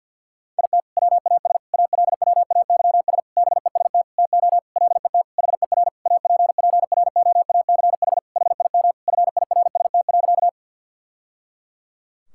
Morse code, heard as a keyed tone, about 52 wpm, 700 Hz. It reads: IT WAS ALWAYS BEST TO LET HER APPROACH HIM FIRST.
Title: Sentences from Top 1000 Words - Review Part 6 - 50wpm